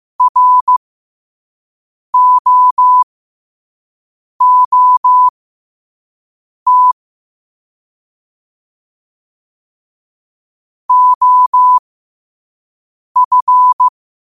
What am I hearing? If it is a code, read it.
Morse code, heard as a keyed tone, about 15 wpm, spaced as for 5 wpm, 1000 Hz. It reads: ROOT OF
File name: qotdfiles / 5wpm